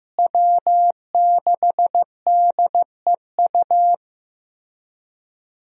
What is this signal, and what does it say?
Morse code, heard as a keyed tone, about 15 wpm, 700 Hz. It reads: W6DEU
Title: USA Call Signs - 5 Characters - Repeated 3x - 15wpm